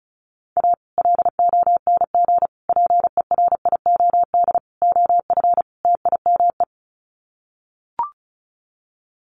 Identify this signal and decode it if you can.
Morse code, heard as a keyed tone, about 34 wpm, 700 Hz. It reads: A LONG PERIOD OF TIME
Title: Sets of 5 Words 35wpm